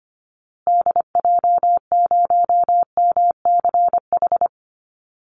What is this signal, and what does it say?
Morse code, heard as a keyed tone, about 25 wpm, 700 Hz. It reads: DJ0MCH
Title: Call Signs Repeated 2x - 25wpm